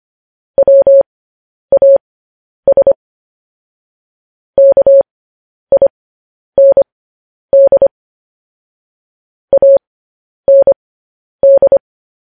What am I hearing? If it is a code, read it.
Morse code, heard as a keyed tone, about 25 wpm, spaced as for 10 wpm, 550 Hz. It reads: WAS KIND AND
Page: news-headlines-in-morse-code-at-10-wpm